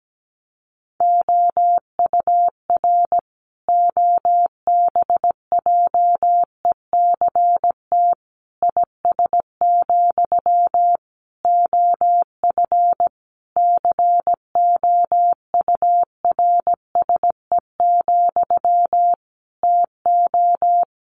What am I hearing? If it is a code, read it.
Morse code, heard as a keyed tone, about 17 wpm, 700 Hz. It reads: OUR OBJECT IS, OF COURSE, TO
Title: Sentences from Top 1500 Words 17wpm